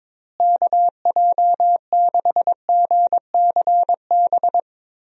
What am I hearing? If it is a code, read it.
Morse code, heard as a keyed tone, about 21 wpm, 700 Hz. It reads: KJ6GCB
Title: USA Call Signs - 6 Characters - Repeated 2x - 22wpm